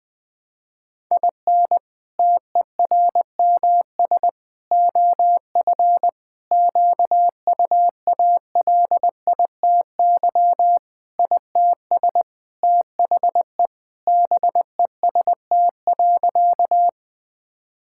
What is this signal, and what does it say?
Morse code, heard as a keyed tone, about 20 wpm, 700 Hz. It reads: IN TERMS OF QUALITY ITS THE BEST.
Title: Sentences with 3-Word Phrases - 20wpm